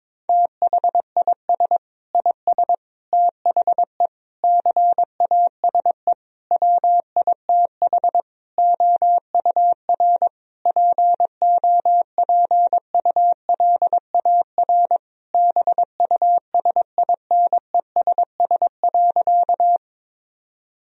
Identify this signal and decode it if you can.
Morse code, heard as a keyed tone, about 22 wpm, 700 Hz. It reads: THIS IS THE CASE WITH OUR POPULAR BUSINESS.
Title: Sentences from Top 1000 Words - Review Part 3 - 22wpm